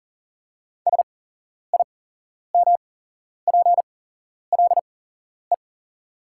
Morse code, heard as a keyed tone, about 40 wpm, spaced as for 11 wpm, 700 Hz. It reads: SIMPLE
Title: 8x Character Spacing - Top 500 Words - 40wpm